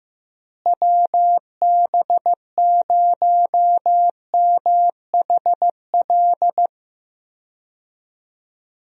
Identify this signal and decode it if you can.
Morse code, heard as a keyed tone, about 15 wpm, 700 Hz. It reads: WB0MHL
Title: USA Call Signs - 6 Characters - Repeated 2x - 15wpm